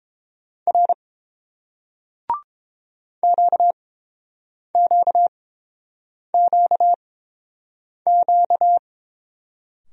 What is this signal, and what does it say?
Morse code, heard as a keyed tone, about 30 wpm, spaced as for 7 wpm, 700 Hz. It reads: R QQQQ